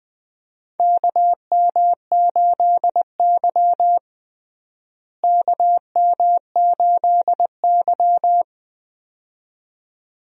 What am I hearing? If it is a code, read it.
Morse code, heard as a keyed tone, about 19 wpm, 700 Hz. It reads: KM8Y KM8Y
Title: USA Call Signs - 4 Characters - Repeated 2x - 20wpm